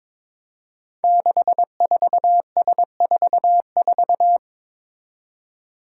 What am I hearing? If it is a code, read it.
Morse code, heard as a keyed tone, about 22 wpm, 700 Hz. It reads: 64S44
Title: Five Challenging Letter-Number Combinations 22wpm